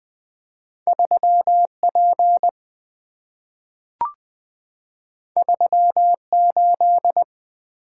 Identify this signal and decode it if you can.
Morse code, heard as a keyed tone, about 20 wpm, 700 Hz. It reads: 3P 38
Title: Two Letter-Number Combinations - 20wpm